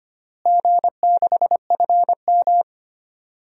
Morse code, heard as a keyed tone, about 25 wpm, 700 Hz. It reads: G6FM
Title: DXCC-100 Call Signs - 4 Characters - Repeated 3x - 25wpm